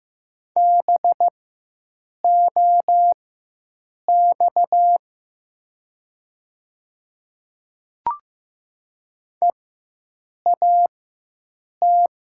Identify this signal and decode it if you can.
Morse code, heard as a keyed tone, about 15 wpm, spaced as for 7 wpm, 700 Hz. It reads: BOX EAT
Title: 4x Character Spacing - Top 500 Words - 15wpm